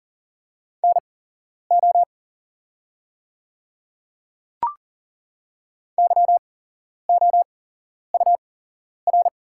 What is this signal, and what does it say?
Morse code, heard as a keyed tone, about 40 wpm, spaced as for 11 wpm, 690 Hz. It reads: NO YOUR
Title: 8x Character Spacing - Top 100 Words - 40wpm